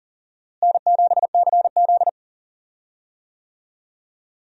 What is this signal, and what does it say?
Morse code, heard as a keyed tone, about 40 wpm, 700 Hz. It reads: N7CZ
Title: USA Call Signs - 4 Characters - Repeated 3x - 40wpm